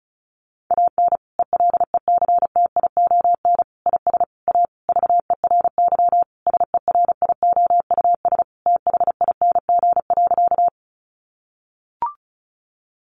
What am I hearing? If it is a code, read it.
Morse code, heard as a keyed tone, about 34 wpm, 700 Hz. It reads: AN ELECTION IS A VERY SERIOUS THING.
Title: Sentences from Top 1500 Words 35wpm